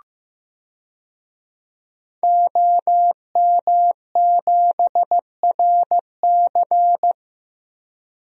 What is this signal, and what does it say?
Morse code, heard as a keyed tone, about 15 wpm, 700 Hz. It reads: OM7RC
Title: DXCC-100 Call Signs - 5 Characters - Repeated 2x - 15wpm